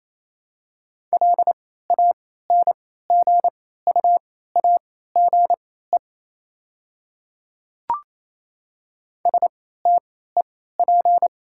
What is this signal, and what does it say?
Morse code, heard as a keyed tone, about 28 wpm, spaced as for 16 wpm, 700 Hz. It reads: LANGUAGE STEP
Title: 3x Character Spacing - Top 500 Words - 28wpm